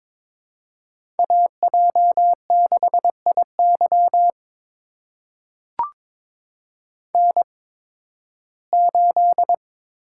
Morse code, heard as a keyed tone, about 22 wpm, spaced as for 14 wpm, 700 Hz. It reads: AJ6IY N8
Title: USA Call Signs - 5 Characters - 8x Character Spacing - 22wpm